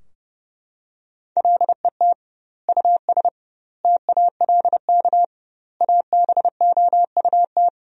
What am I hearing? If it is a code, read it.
Morse code, heard as a keyed tone, about 30 wpm, 700 Hz. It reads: LET US TALK ABOUT
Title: 2x Word-Spacing - Sentences from Top 400 Words - 30wpm